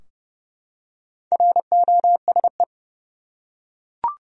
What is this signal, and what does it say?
Morse code, heard as a keyed tone, about 30 wpm, 700 Hz. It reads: ROSE
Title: USA Common Names - Max 4 Letters - 30wpm